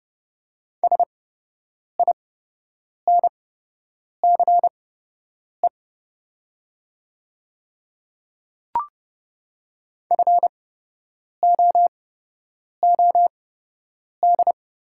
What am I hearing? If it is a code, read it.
Morse code, heard as a keyed tone, about 32 wpm, spaced as for 8 wpm, 700 Hz. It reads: SINCE FOOD
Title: 8x Character Spacing - Top 500 Words - 30wpm